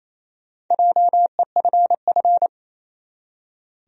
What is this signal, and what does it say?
Morse code, heard as a keyed tone, about 28 wpm, 700 Hz. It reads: JEFF
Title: USA Common Names - Max 5 Letters - 28wpm